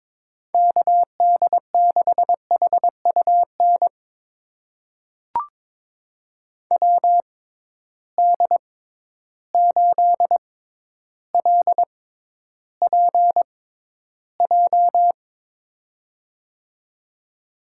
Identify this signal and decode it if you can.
Morse code, heard as a keyed tone, about 21 wpm, spaced as for 11 wpm, 700 Hz. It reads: KD6HUN WD8LPJ
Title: USA Call Signs - 6x Character Spacing - 22wpm